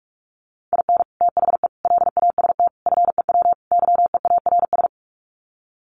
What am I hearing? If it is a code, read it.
Morse code, heard as a keyed tone, about 44 wpm, 700 Hz. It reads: IN THE LAST FEW YEARS
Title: Sets of 5 Words 45wpm